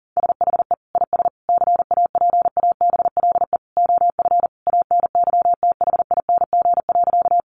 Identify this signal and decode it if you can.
Morse code, heard as a keyed tone, about 37 wpm, 700 Hz. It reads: SHE IS CAPABLE OF ANYTHING.
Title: Sentences from Top 1200 Words 40wpm